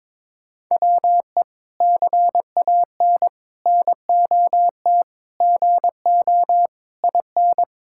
Morse code, heard as a keyed tone, about 21 wpm, 700 Hz. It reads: WE CAN NOT GO IN
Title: Sentences from Top 1000 Words - Review Part 6 - 22wpm